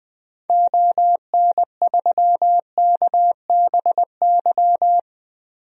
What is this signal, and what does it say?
Morse code, heard as a keyed tone, about 21 wpm, 700 Hz. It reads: ON3KBY
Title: DXCC-100 Call Signs - 6 Characters - 20wpm